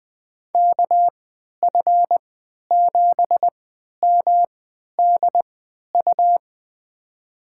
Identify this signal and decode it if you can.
Morse code, heard as a keyed tone, about 20 wpm, spaced as for 11 wpm, 700 Hz. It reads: KF7MDU
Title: USA Call Signs - 6 Characters - 3x Character Spacing - 20wpm